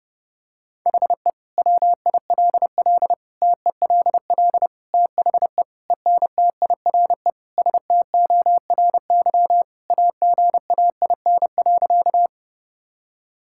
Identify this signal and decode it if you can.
Morse code, heard as a keyed tone, about 32 wpm, 700 Hz. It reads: HE WILL TELL THE ENTIRE STORY AGAIN.
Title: Sentences from Top 1000 Words - Review Part 5 - 30wpm